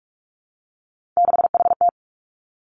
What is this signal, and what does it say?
Morse code, heard as a keyed tone, about 46 wpm, 700 Hz. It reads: 6HT
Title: Three Challenging Letter-Number Combinations 45wpm